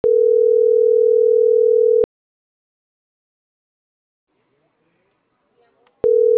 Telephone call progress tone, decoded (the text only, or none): ringback tone